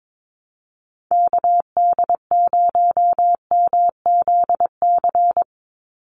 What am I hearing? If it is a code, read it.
Morse code, heard as a keyed tone, about 22 wpm, 700 Hz. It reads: KD0MZC
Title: USA Call Signs 22wpm